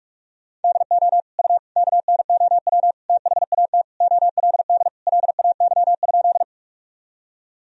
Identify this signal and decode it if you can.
Morse code, heard as a keyed tone, about 46 wpm, 700 Hz. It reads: DO U KNOW THAT OLD LAY?
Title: Sentences from Top 400 Words and Abbreviations 45wpm